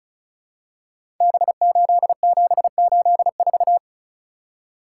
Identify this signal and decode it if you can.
Morse code, heard as a keyed tone, about 36 wpm, 700 Hz. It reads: B8784
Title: Code Groups - 5 Letter:Numbers - 35wpm